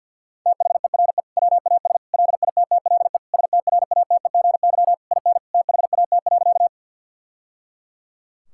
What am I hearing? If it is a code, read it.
Morse code, heard as a keyed tone, about 50 wpm, 700 Hz. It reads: THERE WAS LITTLE STRATEGY IN THAT.
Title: Sentences from Top 1000 Words - Review Part 5 - 50wpm